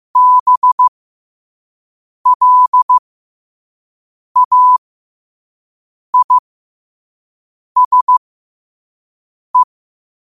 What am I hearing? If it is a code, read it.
Morse code, heard as a keyed tone, about 15 wpm, spaced as for 5 wpm, 1000 Hz. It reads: BLAISE
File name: qotdfiles / 5wpm